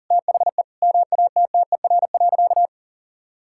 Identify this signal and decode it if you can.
Morse code, heard as a keyed tone, about 40 wpm, 700 Hz. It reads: THE MATTER.
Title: Sentences from Top 1000 Words - Review Part 5 - 40wpm